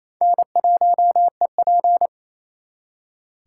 Morse code, heard as a keyed tone, about 36 wpm, 700 Hz. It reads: N1EP